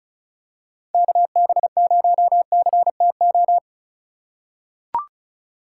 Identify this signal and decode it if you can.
Morse code, heard as a keyed tone, about 36 wpm, 700 Hz. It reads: KB0CTO
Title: USA Call Signs - 6 Characters - 3x Character Spacing - 35wpm